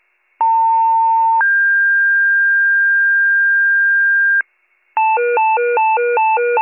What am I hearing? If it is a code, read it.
Two-tone page, heard as a one-second tone, then a three-second tone, about 900 then 1600 Hz, with noise. It is heard from 0.4 to 4.4 s.